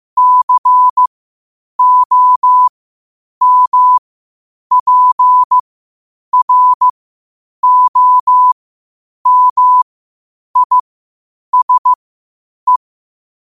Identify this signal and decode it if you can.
Morse code, heard as a keyed tone, about 15 wpm, spaced as for 8 wpm, 1000 Hz. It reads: COMPROMISE
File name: qotdfiles / 8wpm